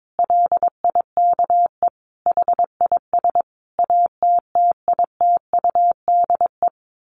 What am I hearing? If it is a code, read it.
Morse code, heard as a keyed tone, about 22 wpm, 700 Hz. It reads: LIKE HIS ATTITUDE
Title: Sentences from Top 1300 Words 22wpm